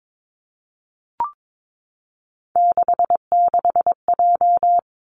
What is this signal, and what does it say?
Morse code, heard as a keyed tone, about 22 wpm, 700 Hz. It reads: 66J
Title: Three Challenging Letter-Number Combinations 22wpm